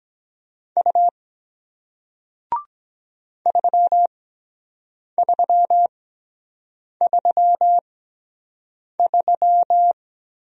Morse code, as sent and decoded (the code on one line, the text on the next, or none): ..- / ...-- ...-- ...-- ...--
U 3333